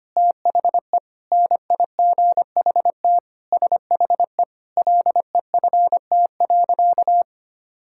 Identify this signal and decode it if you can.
Morse code, heard as a keyed tone, about 25 wpm, 700 Hz. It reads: THE NIGHT SHE LEFT.
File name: practice / 25wpm / Sentences with 4-Word Phrases - 25wpm